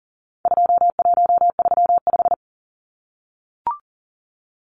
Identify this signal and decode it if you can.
Morse code, heard as a keyed tone, about 40 wpm, 700 Hz. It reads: 2135
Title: Numbers - 4-Digit 40wpm